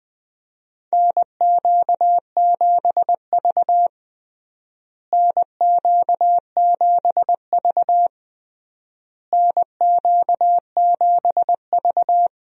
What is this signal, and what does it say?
Morse code, heard as a keyed tone, about 20 wpm, 700 Hz. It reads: NQ7V NQ7V NQ7V
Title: USA Call Signs - 4 Characters - Repeated 3x - 20wpm